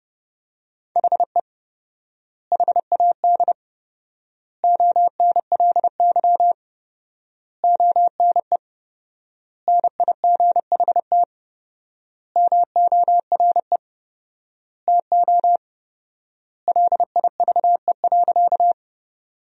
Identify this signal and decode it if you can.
Morse code, heard as a keyed tone, about 30 wpm, 700 Hz. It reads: HE HAD ONLY ONE NIGHT MORE TO LIVE.
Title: 4x Word-Spacing - Sentences from Top 300 Words - 30wpm